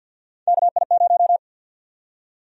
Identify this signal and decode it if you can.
Morse code, heard as a keyed tone, about 51 wpm, 700 Hz. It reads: KI0
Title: USA Call Sign Prefixes - 50wpm